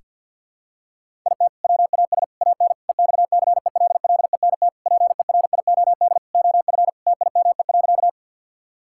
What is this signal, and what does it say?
Morse code, heard as a keyed tone, about 51 wpm, 700 Hz. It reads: IT WAS AN EXCELLENT PERIOD OF TIME.
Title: Sentences from Top 1000 Words - Review Part 5 - 50wpm